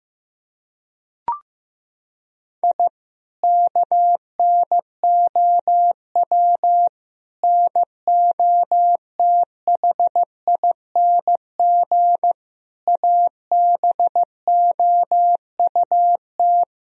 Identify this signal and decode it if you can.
Morse code, heard as a keyed tone, about 15 wpm, 700 Hz. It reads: I KNOW NOTHING ABOUT
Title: Sentences from Top 1000 Words - Review Part 2 - 15wpm